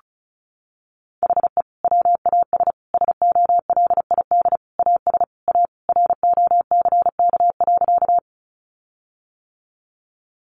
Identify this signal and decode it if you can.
Morse code, heard as a keyed tone, about 34 wpm, 700 Hz. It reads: HE WAS SOLID AS A ROCK.
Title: Sentences from Top 1500 Words 35wpm